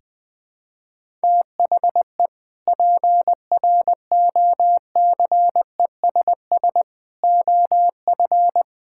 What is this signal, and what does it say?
Morse code, heard as a keyed tone, about 20 wpm, 700 Hz. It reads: THE PROCESS OF
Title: Sentences with 3-Word Phrases - Spoken Beginning - 20wpm